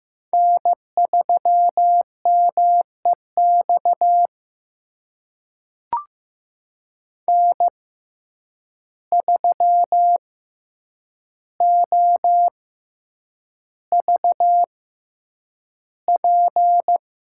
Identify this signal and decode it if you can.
Morse code, heard as a keyed tone, about 15 wpm, spaced as for 8 wpm, 700 Hz. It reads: N3MEX N3OVP
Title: USA Call Signs - 5 Characters - 6x Character Spacing - 15wpm